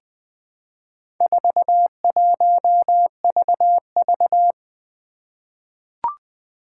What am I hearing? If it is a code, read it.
Morse code, heard as a keyed tone, about 20 wpm, 700 Hz. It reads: 41VV
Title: Four Challenging Letter-Number Combinations 20wpm